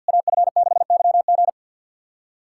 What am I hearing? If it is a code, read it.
Morse code, heard as a keyed tone, about 50 wpm, 700 Hz. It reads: AF6YG
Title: SOTA QSO Practice 50wpm